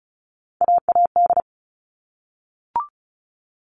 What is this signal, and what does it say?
Morse code, heard as a keyed tone, about 34 wpm, 700 Hz. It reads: AAD
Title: All Cut Numbers - 3-Digit 35wpm